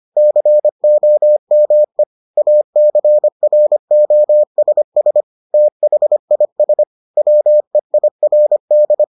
Morse code, heard as a keyed tone, about 25 wpm, 600 Hz. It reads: COME ACROSS THIS WEIRD